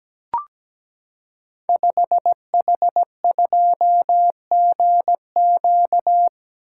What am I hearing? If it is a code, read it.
Morse code, heard as a keyed tone, about 17 wpm, 700 Hz. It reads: 5H2GQ
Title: Code Groups - 5 Letter:Numbers - 17wpm